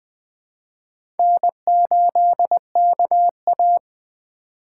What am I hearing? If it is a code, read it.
Morse code, heard as a keyed tone, about 20 wpm, 700 Hz. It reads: N8KA
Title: USA Call Signs - 4x Character Spacing - 20wpm